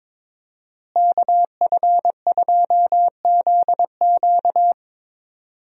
Morse code, heard as a keyed tone, about 22 wpm, 700 Hz. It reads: KF2ZQ
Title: USA Call Signs - 5 Characters - Repeated 3x - 22wpm